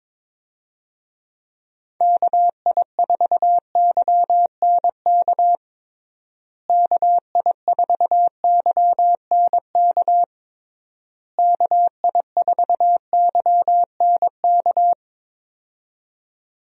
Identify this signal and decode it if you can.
Morse code, heard as a keyed tone, about 21 wpm, 700 Hz. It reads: KI4YNK KI4YNK KI4YNK
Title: USA Call Signs - 6 Characters - Repeated 3x - 22wpm